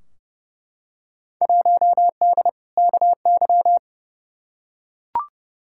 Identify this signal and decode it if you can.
Morse code, heard as a keyed tone, about 30 wpm, 700 Hz. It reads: 1D KY
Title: Field Day Exchanges - 30wpm